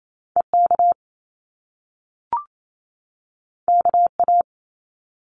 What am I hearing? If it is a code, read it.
Morse code, heard as a keyed tone, about 28 wpm, 700 Hz. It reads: EK KA
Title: Top Two Letter Combinations in Words 28wpm